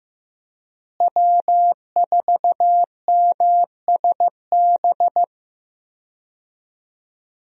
Morse code, heard as a keyed tone, about 15 wpm, 700 Hz. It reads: W4MSB
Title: USA Call Signs - 5 Characters - Repeated 2x - 15wpm